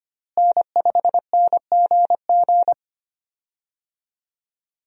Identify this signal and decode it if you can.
Morse code, heard as a keyed tone, about 25 wpm, 700 Hz. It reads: N5NGG
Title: USA Call Signs - 5 Characters - Repeated 3x - 25wpm